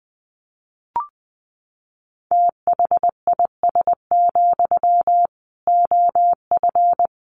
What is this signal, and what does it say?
Morse code, heard as a keyed tone, about 19 wpm, 700 Hz. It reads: THIS, OF
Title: Sentences from Top 1400 Words 20wpm